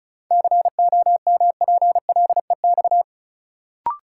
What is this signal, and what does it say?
Morse code, heard as a keyed tone, about 36 wpm, 700 Hz. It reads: COMPLEX
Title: Sentences from Top 900 Words - New Word Drill - 35wpm